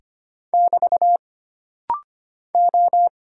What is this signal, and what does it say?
Morse code, heard as a keyed tone, about 25 wpm, 700 Hz. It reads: <BT> O
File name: Single Character (Letter:Number:Punctuation:Prosign) - Rapid-Fire - 25wpm